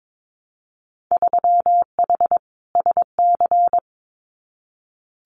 Morse code, heard as a keyed tone, about 22 wpm, 700 Hz. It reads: 3H SC
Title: Winter Field Day Exchanges 22wpm